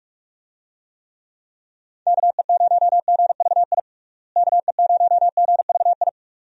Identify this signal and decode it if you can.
Morse code, heard as a keyed tone, about 41 wpm, 700 Hz. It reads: KE0GVI KE0GVI
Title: USA Call Signs - 6 Characters - Repeated 2x - 45wpm